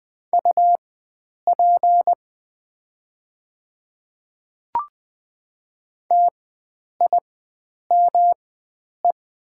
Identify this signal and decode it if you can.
Morse code, heard as a keyed tone, about 20 wpm, spaced as for 9 wpm, 700 Hz. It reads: UP TIME
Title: 4x Character Spacing - Top 100 Words - 20wpm